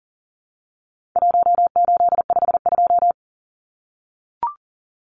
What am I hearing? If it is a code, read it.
Morse code, heard as a keyed tone, about 40 wpm, 700 Hz. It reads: 1852
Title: Numbers - 4-Digit 40wpm